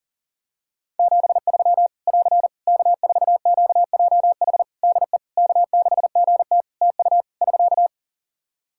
Morse code, heard as a keyed tone, about 40 wpm, 700 Hz. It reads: 73 <AR> K4QJH DE K6GT TU <SK>
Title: Practice QSO - Exchange 4 - Ending - 40wpm